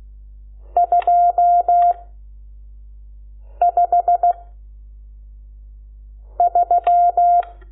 Morse code, heard as a keyed tone, about 16 wpm, 700 Hz. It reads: 2 5 3
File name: Lecciones / CLASE Nº12